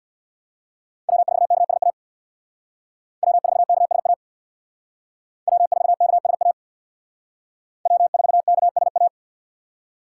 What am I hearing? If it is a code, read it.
Morse code, heard as a keyed tone, about 62 wpm, 700 Hz. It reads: W4KSA W4KSA W4KSA W4KSA
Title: Speed Racing - USA 1x3 Call Signs 50wpm